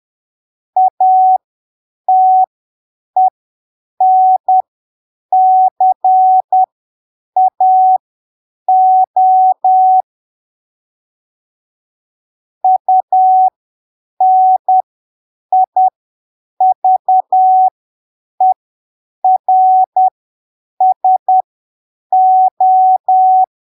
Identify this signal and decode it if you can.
Morse code, heard as a keyed tone, about 10 wpm, spaced as for 7 wpm, 750 Hz. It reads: ATENCAO UNIVERSO